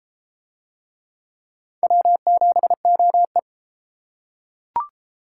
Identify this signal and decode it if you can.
Morse code, heard as a keyed tone, about 29 wpm, 700 Hz. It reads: W7OE